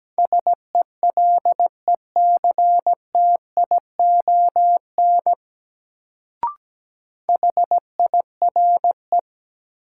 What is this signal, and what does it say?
Morse code, heard as a keyed tone, about 17 wpm, 700 Hz. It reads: SELECTION HIRE
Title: Sentences from Top 1400 Words - New Word Drill - 17wpm